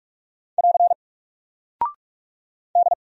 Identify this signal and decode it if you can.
Morse code, heard as a keyed tone, about 48 wpm, 700 Hz. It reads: <AR> D
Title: Single Character (Letter:Number:Punctuation:Prosign) Alt 45wpm